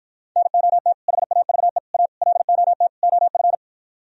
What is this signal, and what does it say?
Morse code, heard as a keyed tone, about 53 wpm, 700 Hz. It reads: NOT HAVE A LOT OF